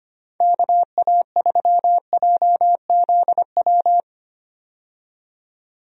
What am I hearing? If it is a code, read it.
Morse code, heard as a keyed tone, about 25 wpm, 700 Hz. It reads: KA3JZW
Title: USA Call Signs - 6 Characters - 25wpm